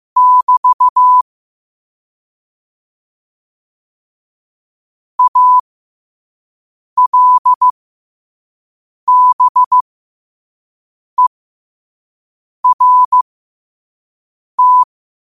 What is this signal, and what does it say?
Morse code, heard as a keyed tone, about 15 wpm, spaced as for 5 wpm, 1000 Hz. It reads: <BT> ALBERT
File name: qotdfiles / 5wpm